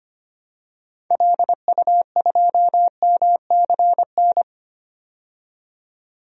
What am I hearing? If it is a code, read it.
Morse code, heard as a keyed tone, about 25 wpm, 700 Hz. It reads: LU2MCN
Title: DXCC-100 Call Signs - 6 Characters - 25wpm